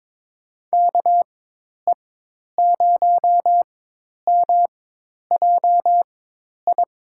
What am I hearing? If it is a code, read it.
Morse code, heard as a keyed tone, about 22 wpm, spaced as for 10 wpm, 700 Hz. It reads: KE0MJI
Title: USA Call Signs - 6 Characters - 4x Character Spacing - 22wpm